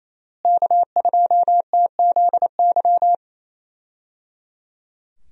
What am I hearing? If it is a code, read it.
Morse code, heard as a keyed tone, about 28 wpm, 700 Hz. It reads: K2TZY
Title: USA Call Signs - 5 Characters - Repeated 3x - 28wpm